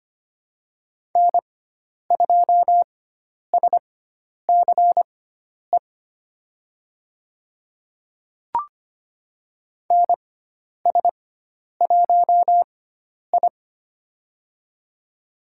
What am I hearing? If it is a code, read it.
Morse code, heard as a keyed tone, about 25 wpm, spaced as for 10 wpm, 700 Hz. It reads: N2SCE NS1I
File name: Lesson 16g - Letter c - Call Signs 25wpm